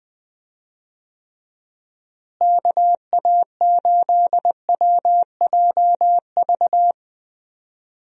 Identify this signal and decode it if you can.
Morse code, heard as a keyed tone, about 20 wpm, 700 Hz. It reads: KA8WJV
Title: USA Call Signs - 6 Characters - Repeated 2x - 20wpm